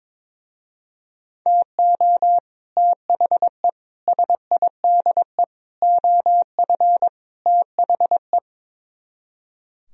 Morse code, heard as a keyed tone, about 22 wpm, 700 Hz. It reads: TO THE SIDE OF THE
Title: Speed-Racing - Sets of 5 Words 22wpm